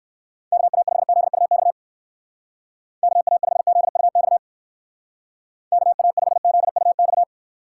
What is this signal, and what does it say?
Morse code, heard as a keyed tone, about 59 wpm, 700 Hz. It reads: KA5ZUX KA5ZUX KA5ZUX